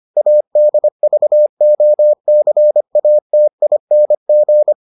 Morse code, heard as a keyed tone, about 25 wpm, 600 Hz. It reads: ADVOCATING